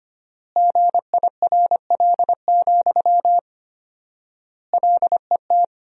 Morse code, heard as a keyed tone, about 25 wpm, 700 Hz. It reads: GIRL, LET
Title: 4x Word-Spacing - Sentences from Top 400 Words - 25wpm